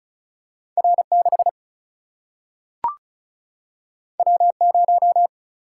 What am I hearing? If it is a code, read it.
Morse code, heard as a keyed tone, about 36 wpm, 700 Hz. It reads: R 6 W 0
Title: Single-Letter Single-Number 35wpm